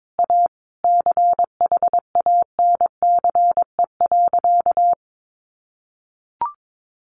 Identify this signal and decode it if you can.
Morse code, heard as a keyed tone, about 22 wpm, 700 Hz. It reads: A CHANCE.
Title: Sentences from Top 1400 Words 22wpm